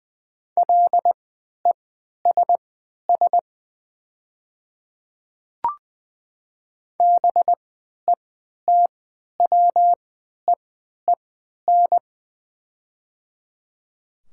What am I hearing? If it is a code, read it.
Morse code, heard as a keyed tone, about 20 wpm, spaced as for 11 wpm, 700 Hz. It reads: LESS BETWEEN
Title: 3x Character Spacing - Top 500 Words - 20wpm